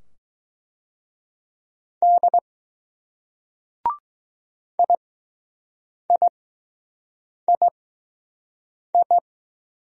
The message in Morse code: D I I I I

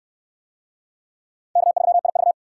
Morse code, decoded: K3IV

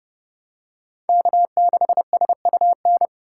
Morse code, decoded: K6SUN

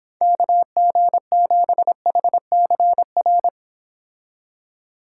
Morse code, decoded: KG7HCR

